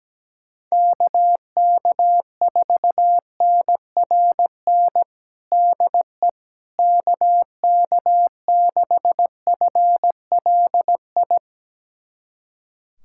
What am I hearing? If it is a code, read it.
Morse code, heard as a keyed tone, about 17 wpm, 700 Hz. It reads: KK4NRN DE KK6FLI
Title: USA Callsign DE Callsign - 6 Characters - 17wpm